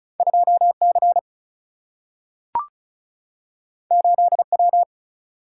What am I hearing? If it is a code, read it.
Morse code, heard as a keyed tone, about 36 wpm, 700 Hz. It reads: 2 C 8 W
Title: Single-Letter Single-Number 35wpm